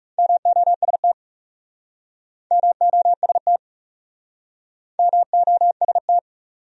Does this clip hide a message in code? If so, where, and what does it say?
Yes: Morse code from 0.2 to 6.2 s, heard as a keyed tone, about 42 wpm, 700 Hz. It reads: MOST MOST MOST